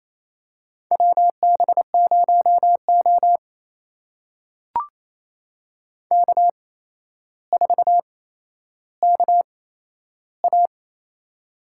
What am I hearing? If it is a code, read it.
Morse code, heard as a keyed tone, about 28 wpm, spaced as for 12 wpm, 700 Hz. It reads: WB0O K4KA
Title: USA Call Signs - 4 Characters - 8x Character Spacing - 28wpm